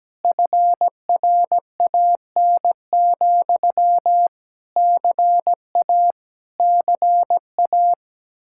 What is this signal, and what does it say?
Morse code, heard as a keyed tone, about 17 wpm, 700 Hz. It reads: FRAN, CA CA
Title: Speed-Racing Common QSO Phrases - 17wpm